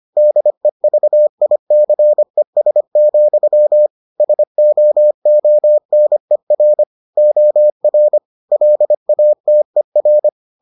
Morse code, heard as a keyed tone, about 25 wpm, 600 Hz. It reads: DEVICES, SOONER OR LATER